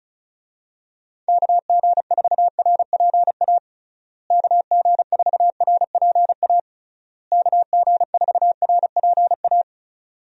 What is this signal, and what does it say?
Morse code, heard as a keyed tone, about 36 wpm, 700 Hz. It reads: KG4RPA KG4RPA KG4RPA
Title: USA Call Signs - 6 Characters - Repeated 3x - 35wpm